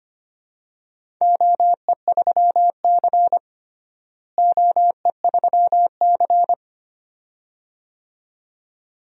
Morse code, heard as a keyed tone, about 25 wpm, 700 Hz. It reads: OE3C OE3C
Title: DXCC-100 Call Signs - 4 Characters - Repeated 2x - 25wpm